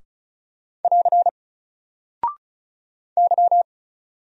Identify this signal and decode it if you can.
Morse code, heard as a keyed tone, about 36 wpm, 700 Hz. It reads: <AR> Y
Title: Single Character (Letter:Number:Punctuation:Prosign) 35wpm